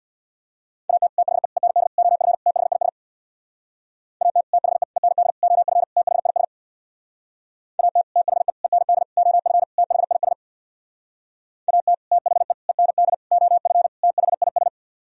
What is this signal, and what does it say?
Morse code, heard as a keyed tone, about 62 wpm, 700 Hz. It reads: AT THE END OF THIS AT THE END OF THIS AT THE END OF THIS AT THE END OF THIS
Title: Speed-Racing - Sets of 5 Words 50wpm